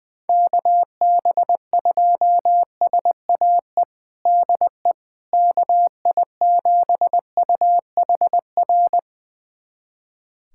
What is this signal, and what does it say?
Morse code, heard as a keyed tone, about 21 wpm, 700 Hz. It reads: KB2SAE DE KI7UHR
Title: USA Callsign DE Callsign - 6 Characters - 20wpm